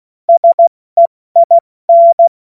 Morse code, heard as a keyed tone, about 16 wpm, 680 Hz. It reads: SEIN
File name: Morsekurs / Witze / Witz 10. August 2021, 70 F80